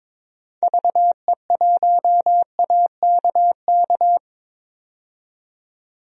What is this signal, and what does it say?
Morse code, heard as a keyed tone, about 21 wpm, 700 Hz. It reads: VE1AKK
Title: DXCC-100 Call Signs - 6 Characters - 22wpm